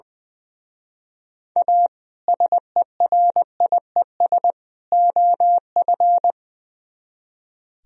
Morse code, heard as a keyed tone, about 22 wpm, 700 Hz. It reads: A SERIES OF
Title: Speed-Racing - Sets of 3 Words 20wpm